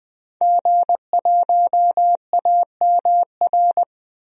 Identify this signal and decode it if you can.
Morse code, heard as a keyed tone, about 19 wpm, 700 Hz. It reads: G1AMR
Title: DXCC-100 Call Signs - 5 Characters - Repeated 3x - 20wpm